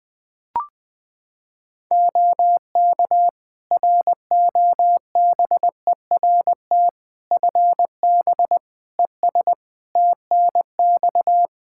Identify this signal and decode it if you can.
Morse code, heard as a keyed tone, about 20 wpm, 700 Hz. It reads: OK ROBERT FB ES TNX
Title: K1USN SST QSO Practice 20wpm